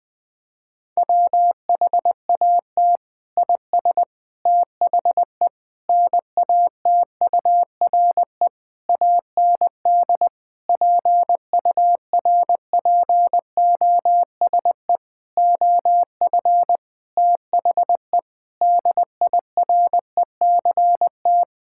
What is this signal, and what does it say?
Morse code, heard as a keyed tone, about 20 wpm, 700 Hz. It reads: WHAT IS THE NATURE AND PURPOSE OF THE DIRECT
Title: Sentences from Top 1000 Words - Review Part 5 - 20wpm